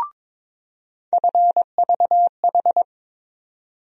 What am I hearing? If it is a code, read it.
Morse code, heard as a keyed tone, about 22 wpm, 700 Hz. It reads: FVH
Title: USA Call Sign Postfixes - 3 Letters 22wpm